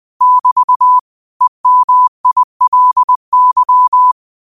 Morse code, heard as a keyed tone, about 20 wpm, 1000 Hz. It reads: <BT> EMILY
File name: qotdfiles / 20wpm